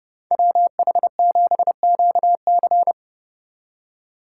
Morse code, heard as a keyed tone, about 36 wpm, 700 Hz. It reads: WH7QC
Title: Speed Racing - USA 2x2 Call Signs 30wpm